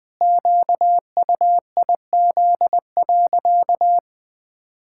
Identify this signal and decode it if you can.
Morse code, heard as a keyed tone, about 20 wpm, 700 Hz. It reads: QUIZ.
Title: Pangram Challenge v2 - 20wpm